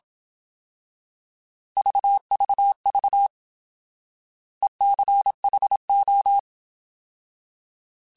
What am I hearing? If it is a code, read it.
Morse code, heard as a keyed tone, about 27 wpm, 800 Hz. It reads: VVV ECHO
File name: Alphabetical / Alphabetical mp3s / Es @25